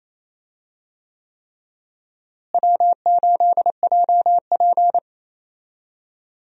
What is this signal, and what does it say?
Morse code, heard as a keyed tone, about 28 wpm, 700 Hz. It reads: W8JP
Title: USA Call Signs - 4 Characters - Repeated 3x - 28wpm